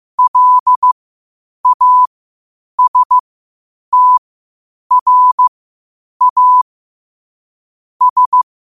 Morse code, heard as a keyed tone, about 15 wpm, spaced as for 8 wpm, 1000 Hz. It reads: LASTRA S
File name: qotdfiles / 8wpm